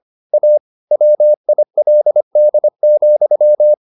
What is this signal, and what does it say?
Morse code, heard as a keyed tone, about 25 wpm, 600 Hz. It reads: A WILD,
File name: the old man and the sea part 1 25wpm